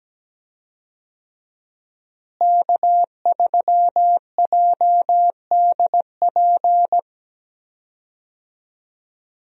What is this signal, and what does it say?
Morse code, heard as a keyed tone, about 17 wpm, 700 Hz. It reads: K3JDP
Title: USA Call Signs - 5 Characters - Repeated 2x - 17wpm